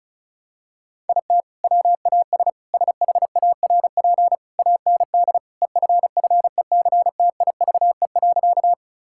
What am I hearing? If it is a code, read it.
Morse code, heard as a keyed tone, about 34 wpm, 700 Hz. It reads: IT WAS SHARP AND EFFECTIVE.
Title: Sentences from Top 1600 Words 35wpm